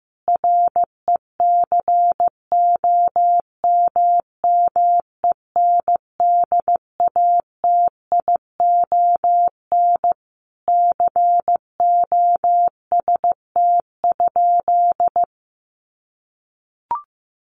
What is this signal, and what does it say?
Morse code, heard as a keyed tone, about 15 wpm, 700 Hz. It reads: RECOMMENDATION COST?
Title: Sentences from Top 1400 Words 15wpm